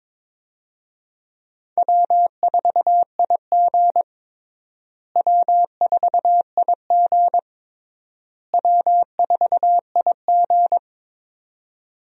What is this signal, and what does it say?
Morse code, heard as a keyed tone, about 22 wpm, 700 Hz. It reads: W4IG W4IG W4IG
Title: USA Call Signs - 4 Characters - Repeated 3x - 22wpm